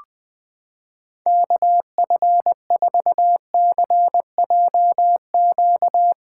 Morse code, heard as a keyed tone, about 20 wpm, 700 Hz. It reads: KF4CJQ
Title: USA Callsign DE Callsign - 6 Characters - 20wpm